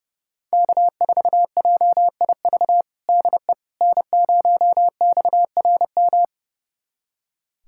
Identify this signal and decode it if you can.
Morse code, heard as a keyed tone, about 29 wpm, 700 Hz. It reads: K4JIV DE N0XRM
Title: USA Callsign DE Callsign - 5 Characters - 30wpm